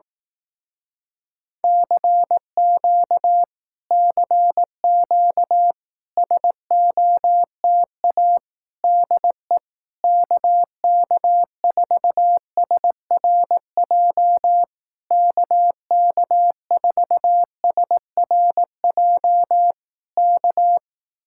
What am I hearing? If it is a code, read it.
Morse code, heard as a keyed tone, about 18 wpm, 700 Hz. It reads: CQ CQ SOTA DE KK4SRJ KK4SRJ K
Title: Speed-Racing Common QSO Phrases Encore - 15wpm